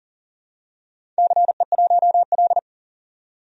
Morse code, heard as a keyed tone, about 39 wpm, 700 Hz. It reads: CE1L
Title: DXCC-100 Call Signs - 4 Characters - 40wpm